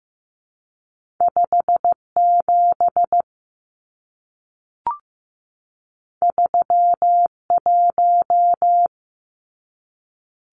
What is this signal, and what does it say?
Morse code, heard as a keyed tone, about 15 wpm, 700 Hz. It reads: 57 31
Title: Numbers - 2-Digit 15wpm